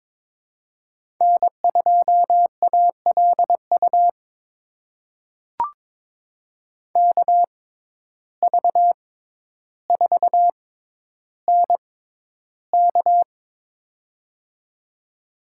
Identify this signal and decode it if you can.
Morse code, heard as a keyed tone, about 22 wpm, spaced as for 11 wpm, 700 Hz. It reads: N2ALU KV4NK
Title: USA Call Signs - 5 Characters - 6x Character Spacing - 22wpm